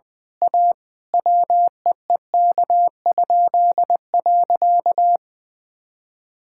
Morse code, heard as a keyed tone, about 19 wpm, 700 Hz. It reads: A WEEK?.
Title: Pangram Challenge v2 - 20wpm